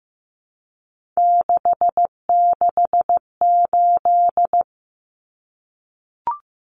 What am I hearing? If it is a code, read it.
Morse code, heard as a keyed tone, about 15 wpm, 700 Hz. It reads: 668
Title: Common Cut Numbers - 3-Digit 15wpm